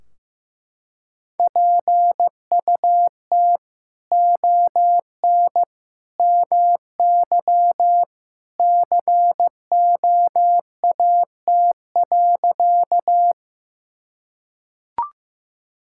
Morse code, heard as a keyed tone, about 15 wpm, 700 Hz. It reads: PUT ON MY COAT.
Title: Sentences from Top 1600 Words 15wpm